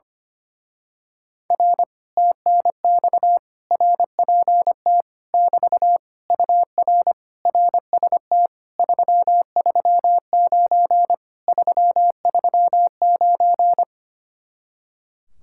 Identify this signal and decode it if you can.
Morse code, heard as a keyed tone, about 24 wpm, 700 Hz. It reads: R TNX RPT <BT> UR RST 339 339
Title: Speed-Racing Common QSO Phrases - 25wpm